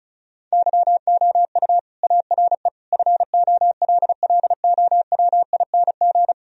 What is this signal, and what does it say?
Morse code, heard as a keyed tone, about 34 wpm, 700 Hz. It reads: YOU ARE FOLLOWING
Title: Sentences from Top 1000 Words - Review Part 4 - 35wpm